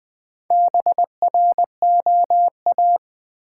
Morse code, heard as a keyed tone, about 20 wpm, 700 Hz. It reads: BROA
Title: Build A Word - 4 Letters At A Time - 11-Letter Words - 20wpm